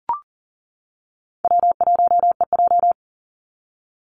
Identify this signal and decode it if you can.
Morse code, heard as a keyed tone, about 40 wpm, 700 Hz. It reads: W1EJ
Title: NA SOTA Activator Callsigns 40wpm